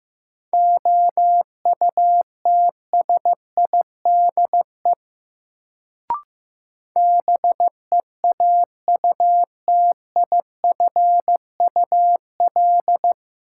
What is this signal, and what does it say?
Morse code, heard as a keyed tone, about 15 wpm, 700 Hz. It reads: OUTSIDE BEAUTIFUL
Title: Sentences from Top 900 Words - New Word Drill - 15wpm